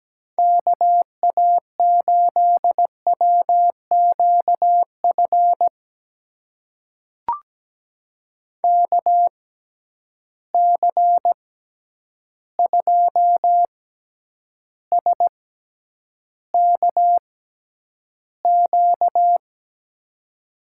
Morse code, heard as a keyed tone, about 17 wpm, spaced as for 9 wpm, 700 Hz. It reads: KA8WQF KC2SKQ
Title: USA Call Signs - 6 Characters - 6x Character Spacing - 17wpm